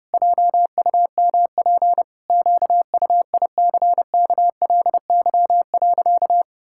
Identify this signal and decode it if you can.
Morse code, heard as a keyed tone, about 30 wpm, 700 Hz. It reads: JUMP QUICKLY.
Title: Pangram Challenge v2 - 30wpm